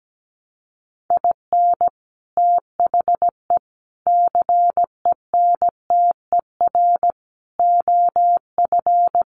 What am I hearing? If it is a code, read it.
Morse code, heard as a keyed tone, about 17 wpm, 700 Hz. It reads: IN THE CENTER OF
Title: Sets of 4 Words 17wpm